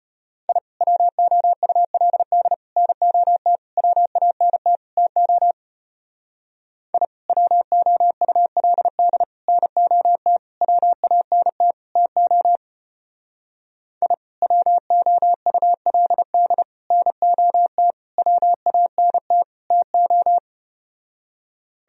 Morse code, heard as a keyed tone, about 33 wpm, 700 Hz. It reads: I WOULD NOT WANT TO I WOULD NOT WANT TO I WOULD NOT WANT TO